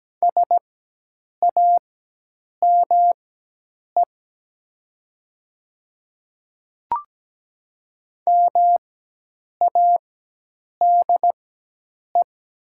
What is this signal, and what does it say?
Morse code, heard as a keyed tone, about 17 wpm, spaced as for 8 wpm, 700 Hz. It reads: SAME MADE
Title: 4x Character Spacing - Top 200 Words - 17wpm